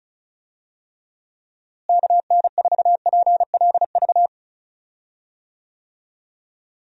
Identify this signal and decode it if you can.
Morse code, heard as a keyed tone, about 36 wpm, 700 Hz. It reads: KN4PLV
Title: USA Call Signs - 6 Characters - Repeated 3x - 35wpm